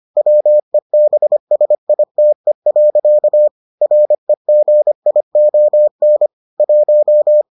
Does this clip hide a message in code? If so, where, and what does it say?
Yes: Morse code from 0.2 to 7.4 s, heard as a keyed tone, about 25 wpm, 600 Hz. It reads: WEBSITE. REGION 1